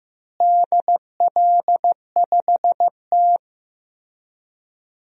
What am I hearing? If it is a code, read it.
Morse code, heard as a keyed tone, about 15 wpm, 700 Hz. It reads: DL5T
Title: DXCC-100 Call Signs - 4 Characters - Repeated 3x - 15wpm